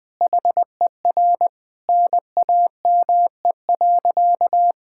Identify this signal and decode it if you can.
Morse code, heard as a keyed tone, about 20 wpm, 700 Hz. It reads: HER NAME.
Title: Sentences with 4-Word Phrases - 20wpm